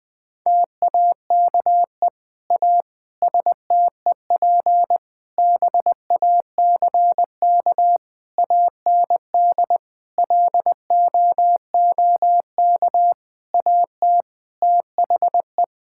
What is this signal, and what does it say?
Morse code, heard as a keyed tone, about 21 wpm, 700 Hz. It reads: TAKE A STEP BACK AND LOOK AT THE
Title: Sentences with 7-Word Phrases - Spoken Beginning - 20wpm